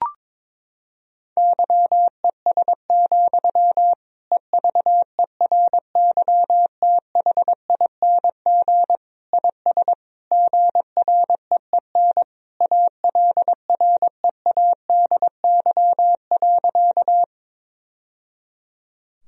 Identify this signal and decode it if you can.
Morse code, heard as a keyed tone, about 21 wpm, 700 Hz. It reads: YES, EVERYTHING IS GREEN ALREADY.
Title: Sentences from Top 1000 Words - Review Part 2 - 22wpm